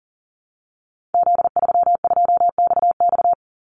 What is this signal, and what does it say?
Morse code, heard as a keyed tone, about 40 wpm, 700 Hz. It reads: Z32XX
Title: Call Signs Repeated 2x - 40wpm